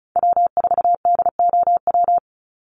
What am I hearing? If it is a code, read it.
Morse code, heard as a keyed tone, about 34 wpm, 700 Hz. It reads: W4DOW
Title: NA SOTA Chaser Callsigns 35wpm